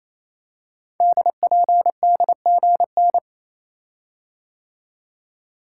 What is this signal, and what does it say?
Morse code, heard as a keyed tone, about 29 wpm, 700 Hz. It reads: DPDGN
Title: Code Groups - 5 Letters - 28wpm